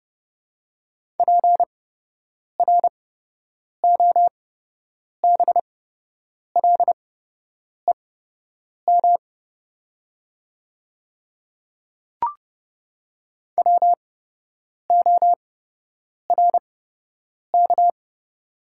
Morse code, heard as a keyed tone, about 30 wpm, spaced as for 8 wpm, 700 Hz. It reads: PROBLEM WORK
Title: 8x Character Spacing - Top 500 Words - 30wpm